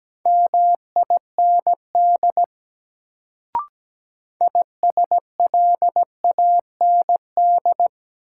Morse code, heard as a keyed tone, about 17 wpm, 700 Hz. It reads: MIND ISLAND
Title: Sentences from Top 500 Words - New Word Drill - 17wpm